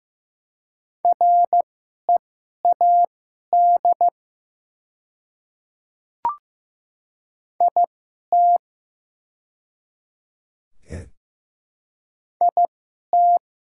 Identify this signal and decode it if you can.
Morse code, heard as a keyed tone, about 15 wpm, spaced as for 11 wpm, 700 Hz. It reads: READ IT IT
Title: 2x Character Spacing - Top 200 Words - 15wpm